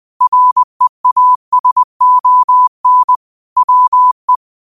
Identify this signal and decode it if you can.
Morse code, heard as a keyed tone, about 20 wpm, 1000 Hz. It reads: REASON WE
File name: qotdfiles / 20wpm